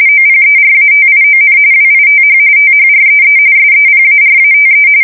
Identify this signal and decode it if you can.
RTTY, about 45.45 baud. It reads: OPERATION WITH THREE SEPARATE
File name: fsk170 / USS-Massachusetts